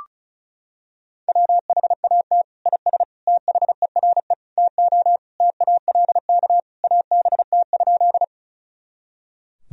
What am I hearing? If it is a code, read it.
Morse code, heard as a keyed tone, about 36 wpm, 700 Hz. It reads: WHAT IS THERE TO TALK ABT?
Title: Sentences from Top 400 Words and Abbreviations 35wpm